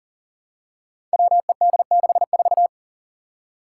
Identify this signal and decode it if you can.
Morse code, heard as a keyed tone, about 40 wpm, 700 Hz. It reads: WED64
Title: Five Challenging Letter-Number Combinations 40wpm